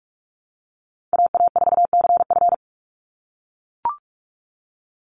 Morse code, heard as a keyed tone, about 44 wpm, 700 Hz. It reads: AA4CF
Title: Call Signs - 45wpm